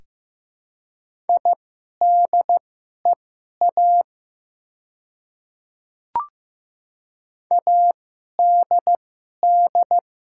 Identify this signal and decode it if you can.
Morse code, heard as a keyed tone, about 15 wpm, spaced as for 11 wpm, 700 Hz. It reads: IDEA ADD
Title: 2x Character Spacing - Top 300 Words - 15wpm